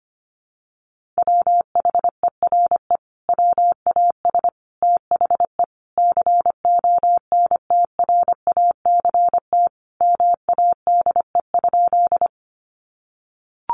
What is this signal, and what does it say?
Morse code, heard as a keyed tone, about 25 wpm, 700 Hz. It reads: WHERE WAS THE CONTRACT MADE?
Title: Sentences from Top 1400 Words 25wpm